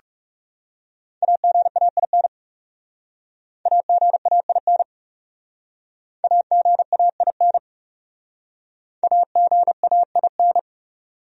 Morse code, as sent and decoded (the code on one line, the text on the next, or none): .- --. .- .. -. / .- --. .- .. -. / .- --. .- .. -. / .- --. .- .. -.
AGAIN AGAIN AGAIN AGAIN